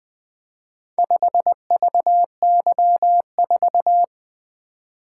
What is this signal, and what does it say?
Morse code, heard as a keyed tone, about 20 wpm, 700 Hz. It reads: 5VY4
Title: Four Letter-Number Combinations - 20wpm